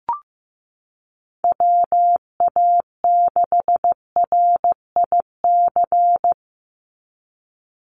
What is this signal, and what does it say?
Morse code, heard as a keyed tone, about 15 wpm, 700 Hz. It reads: WA6RIC
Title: NA SOTA Chaser Callsigns 15wpm